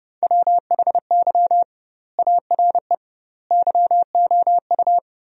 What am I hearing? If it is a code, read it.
Morse code, heard as a keyed tone, about 32 wpm, 700 Hz. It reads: WHY ARE YOU
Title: 2x Word-Spacing - Sentences from Top 200 Words - 30wpm